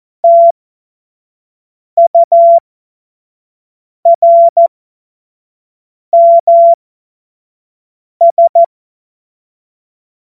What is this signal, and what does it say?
Morse code, heard as a keyed tone, about 14 wpm, spaced as for 5 wpm, 680 Hz. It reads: TURMS
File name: Morsekurs / Witze / Witz 07.November 2023 25 F70